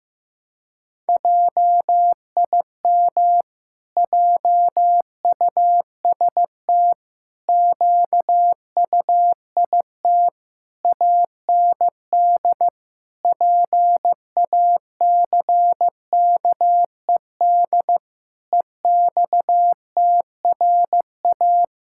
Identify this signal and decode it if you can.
Morse code, heard as a keyed tone, about 16 wpm, 700 Hz. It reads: JIM JUST QUIT AND PACKED EXTRA
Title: Pangram Challenge v2 - 15wpm